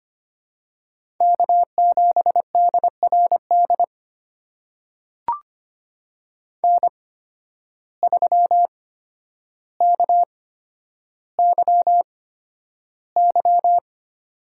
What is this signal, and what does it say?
Morse code, heard as a keyed tone, about 25 wpm, spaced as for 11 wpm, 700 Hz. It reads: K7DRD N3KYY
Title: USA Call Signs - 5 Characters - 8x Character Spacing - 25wpm